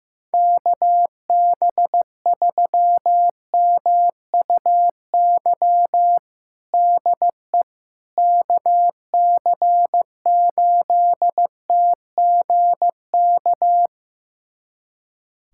Morse code, heard as a keyed tone, about 15 wpm, 700 Hz. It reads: KB3MUY DE KC8TGK
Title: USA Callsign DE Callsign - 6 Characters - 15wpm